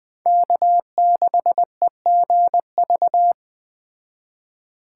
Morse code, heard as a keyed tone, about 20 wpm, 700 Hz. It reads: K6EGV
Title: Speed Racing - USA Call Signs 17wpm